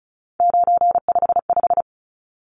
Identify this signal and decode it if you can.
Morse code, heard as a keyed tone, about 34 wpm, 700 Hz. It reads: 955
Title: Numbers - 3-Digit 35wpm